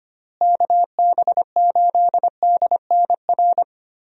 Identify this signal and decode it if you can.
Morse code, heard as a keyed tone, about 25 wpm, 700 Hz. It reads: KB8DNR
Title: USA Call Signs - 6 Characters - Repeated 2x - 25wpm